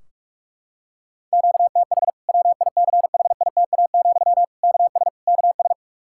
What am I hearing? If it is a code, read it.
Morse code, heard as a keyed tone, about 46 wpm, 700 Hz. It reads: QTH WICHITA, KS KS
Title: K1USN SST QSO Practice 45wpm